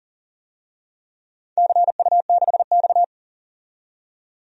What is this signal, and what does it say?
Morse code, heard as a keyed tone, about 38 wpm, 700 Hz. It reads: CU6X
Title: DXCC-100 Call Signs - 4 Characters - 40wpm